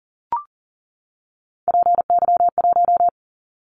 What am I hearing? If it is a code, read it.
Morse code, heard as a keyed tone, about 40 wpm, 700 Hz. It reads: PY1
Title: Callsign Prefixes for Top 100 DXCC - 40wpm